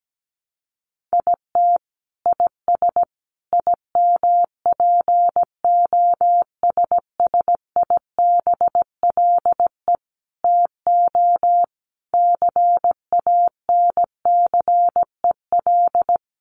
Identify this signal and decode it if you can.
Morse code, heard as a keyed tone, about 17 wpm, 700 Hz. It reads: IT IS IMPOSSIBLE TO CANCEL
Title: Sentences from Top 1200 Words 17wpm